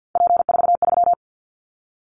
Morse code, heard as a keyed tone, about 50 wpm, 700 Hz. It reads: P43
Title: Callsign Prefixes for Top 100 DXCC - 50wpm